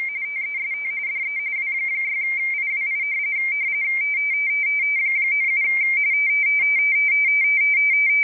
RTTY, about 45.45 baud. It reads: TEST DE W6IWI 97
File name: rtty / websdr recording 14086.0kHz